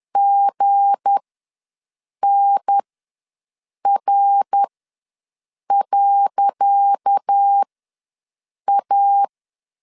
Morse code, heard as a keyed tone, about 10 wpm, spaced as for 6 wpm, 800 Hz. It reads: GNR.A
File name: LEZIONI / 10 SIGNS / 10C DOT ws 10wpm 4213